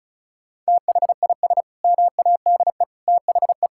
Morse code, heard as a keyed tone, about 36 wpm, 700 Hz. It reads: THIS MADE THE
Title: Sentences from Top 1000 Words - Review Part 6 - 35wpm